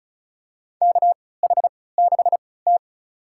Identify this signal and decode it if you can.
Morse code, heard as a keyed tone, about 36 wpm, spaced as for 20 wpm, 700 Hz. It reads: KH6T
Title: USA Call Signs - 4 Characters - 3x Character Spacing - 35wpm